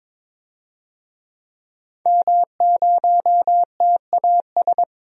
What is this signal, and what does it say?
Morse code, heard as a keyed tone, about 22 wpm, 700 Hz. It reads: M0TAS
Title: DXCC-100 Call Signs - 5 Characters - 22wpm